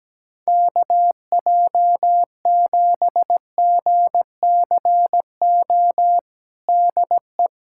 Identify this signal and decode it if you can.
Morse code, heard as a keyed tone, about 17 wpm, 700 Hz. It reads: KJ7GCO DE
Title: USA Callsign DE Callsign 17wpm